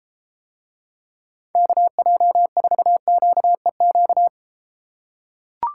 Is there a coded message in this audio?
Yes: Morse code from 1.6 to 4.3 s, heard as a keyed tone, about 32 wpm, 700 Hz. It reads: KJ4QEQ